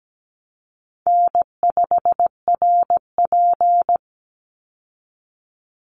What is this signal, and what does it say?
Morse code, heard as a keyed tone, about 17 wpm, 700 Hz. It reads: N5RP
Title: Easy Call Signs Repeated 3x - 17wpm